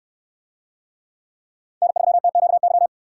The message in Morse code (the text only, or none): N3TQY